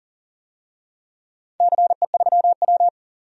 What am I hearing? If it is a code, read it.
Morse code, heard as a keyed tone, about 40 wpm, 700 Hz. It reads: CE3W
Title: DXCC-100 Call Signs - 4 Characters - 40wpm